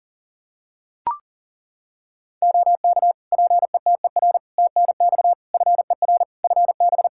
Morse code, heard as a keyed tone, about 40 wpm, 700 Hz. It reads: OK PETER TNX FER FB
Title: K1USN SST QSO Practice 40wpm